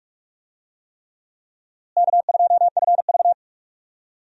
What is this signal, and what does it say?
Morse code, heard as a keyed tone, about 46 wpm, 700 Hz. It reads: K2FV
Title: USA Call Signs - 4 Characters - Repeated 3x - 45wpm